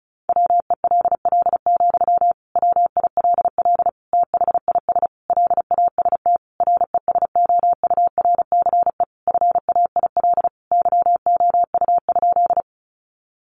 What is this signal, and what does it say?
Morse code, heard as a keyed tone, about 34 wpm, 700 Hz. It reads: WELL, WILL THIS LAST RESOURCE FAIL YOU?
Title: Sentences from Top 1300 Words 35wpm